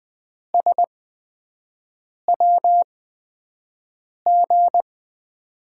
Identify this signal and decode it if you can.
Morse code, heard as a keyed tone, about 20 wpm, spaced as for 5 wpm, 700 Hz. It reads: SWG
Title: USA Call Signs - 5 Characters - 8x Character Spacing - 20wpm